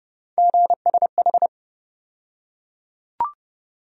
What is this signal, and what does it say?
Morse code, heard as a keyed tone, about 30 wpm, 700 Hz. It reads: GSH
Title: USA Call Sign Postfixes - 3 Letters 30wpm